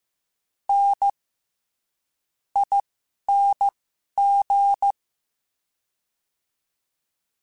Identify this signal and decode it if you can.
Morse code, heard as a keyed tone, about 15 wpm, spaced as for 11 wpm, 800 Hz. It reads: N ING
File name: Special INGs-mp3s / Special INGs-mp3s / SlowToFaster ing @10